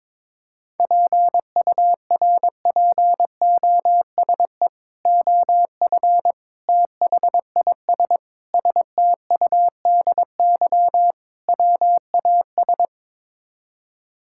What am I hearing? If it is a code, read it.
Morse code, heard as a keyed tone, about 21 wpm, 700 Hz. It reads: PURPOSE OF THIS STUDY WAS
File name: practice / 22wpm / Build A Phrase - 5-Word Phrases - 22wpm